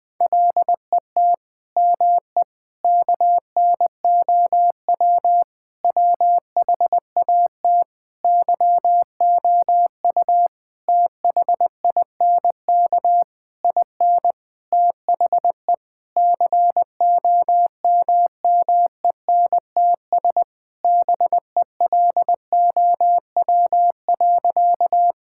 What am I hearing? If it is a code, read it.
Morse code, heard as a keyed tone, about 20 wpm, 700 Hz. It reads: LET ME KNOW WHAT YOU THINK IN THE COMMENTS BELOW.
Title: Sentences with 9-Word Phrases - 20wpm